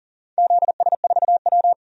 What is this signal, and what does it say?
Morse code, heard as a keyed tone, about 40 wpm, 700 Hz. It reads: ZS4W
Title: DXCC-100 Call Signs - 4 Characters - 40wpm